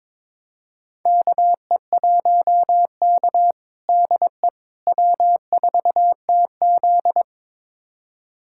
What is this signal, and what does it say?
Morse code, heard as a keyed tone, about 22 wpm, 700 Hz. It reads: KE1K DE W4TZ
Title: USA Callsign DE Callsign - 4 Characters - 22wpm